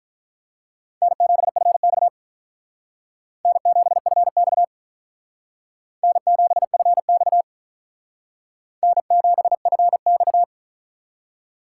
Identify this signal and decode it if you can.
Morse code, heard as a keyed tone, about 42 wpm, 700 Hz. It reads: N7FX N7FX N7FX N7FX
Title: Speed Racing - USA Call Signs 35wpm